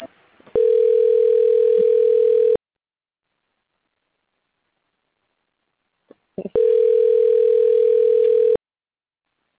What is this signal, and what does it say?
Ringback tone (North America), 440 + 480 Hz.